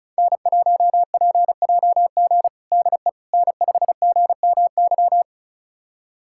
Morse code, heard as a keyed tone, about 34 wpm, 700 Hz. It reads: N1PJG DE N5GMY
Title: USA Callsign DE Callsign - 5 Characters - 35wpm